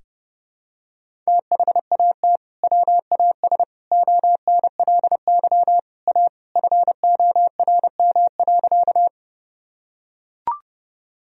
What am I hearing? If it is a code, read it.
Morse code, heard as a keyed tone, about 32 wpm, 700 Hz. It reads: THAT WAS ONLY A FORM.
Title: Sentences from Top 1000 Words - Review Part 1 - 30wpm